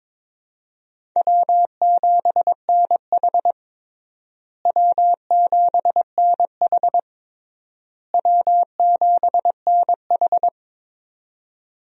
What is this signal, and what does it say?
Morse code, heard as a keyed tone, about 22 wpm, 700 Hz. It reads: W7NH W7NH W7NH
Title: USA Call Signs - 4 Characters - Repeated 3x - 22wpm